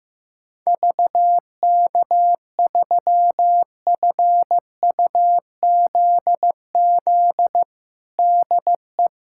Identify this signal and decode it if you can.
Morse code, heard as a keyed tone, about 15 wpm, 700 Hz. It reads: VK3FUZZ DE
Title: USA Callsign DE Callsign 15wpm